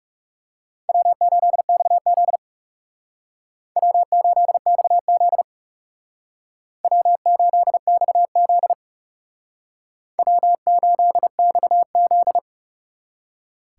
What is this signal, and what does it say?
Morse code, heard as a keyed tone, about 36 wpm, 700 Hz. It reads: W8XZ W8XZ W8XZ W8XZ
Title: Speed Racing - USA 1x2 Call Signs 30wpm